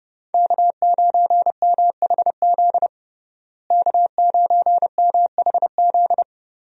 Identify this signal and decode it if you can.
Morse code, heard as a keyed tone, about 30 wpm, 700 Hz. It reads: K9MHZ K9MHZ
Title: USA Call Signs - 5 Characters - Repeated 2x - 30wpm